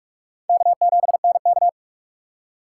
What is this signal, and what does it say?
Morse code, heard as a keyed tone, about 46 wpm, 700 Hz. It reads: K7NK
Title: USA Call Signs - 4 Characters - 2x Character Spacing - 45wpm